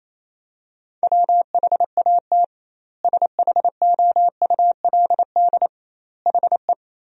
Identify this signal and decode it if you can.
Morse code, heard as a keyed tone, about 27 wpm, 700 Hz. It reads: WHAT SHOULD HE
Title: 2x Word-Spacing - Sentences from Top 300 Words - 28wpm